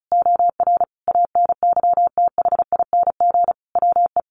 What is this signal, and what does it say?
Morse code, heard as a keyed tone, about 34 wpm, 700 Hz. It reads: OR ANYTHING WE
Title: Sentences from Top 1400 Words 35wpm